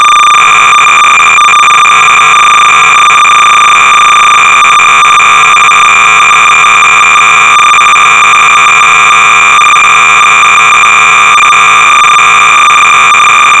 Cassette data (Kansas City type).
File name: Draw v 1.2 (1982)-BASIC-Sorcerer-300baud